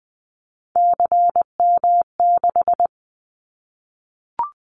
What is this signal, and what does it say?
Morse code, heard as a keyed tone, about 20 wpm, 700 Hz. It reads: CM6
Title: Callsign Prefixes for Top 100 DXCC - 20wpm